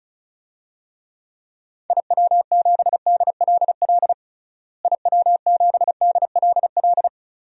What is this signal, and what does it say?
Morse code, heard as a keyed tone, about 36 wpm, 700 Hz. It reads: IW7DLL IW7DLL
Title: DXCC-100 Call Signs - 6 Characters - Repeated 3x - 35wpm